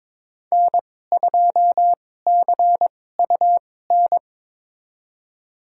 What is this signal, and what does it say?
Morse code, heard as a keyed tone, about 22 wpm, spaced as for 16 wpm, 700 Hz. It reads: N2CUN
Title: USA Call Signs - 2x Character Spacing - 22wpm